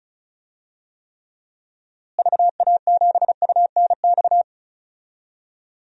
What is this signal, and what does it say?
Morse code, heard as a keyed tone, about 36 wpm, 700 Hz. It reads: VA7UNX
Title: DXCC-100 Call Signs - 6 Characters - Repeated 2x - 35wpm